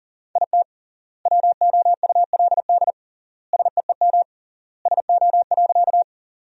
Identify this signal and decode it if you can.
Morse code, heard as a keyed tone, about 37 wpm, 700 Hz. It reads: IT WOULD SEEM SO.
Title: 3x Word-Spacing - Sentences from Top 300 Words - 40wpm